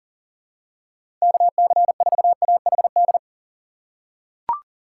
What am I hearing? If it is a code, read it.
Morse code, heard as a keyed tone, about 40 wpm, 700 Hz. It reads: KC4AHD
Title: USA Call Signs - 6 Characters - 2x Character Spacing - 40wpm